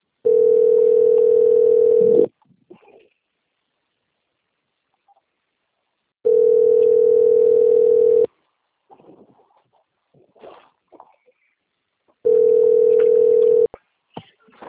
Ringback tone (North America), 440 + 480 Hz.